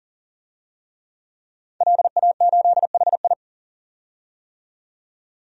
Morse code, heard as a keyed tone, about 40 wpm, 700 Hz. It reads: LA8HI
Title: DXCC-100 Call Signs - 5 Characters - 40wpm